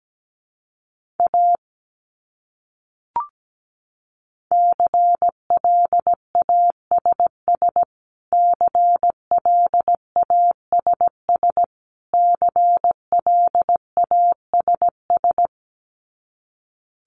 Morse code, heard as a keyed tone, about 17 wpm, 700 Hz. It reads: A CLASS CLASS CLASS
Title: Top 500 Words Repeated 17wpm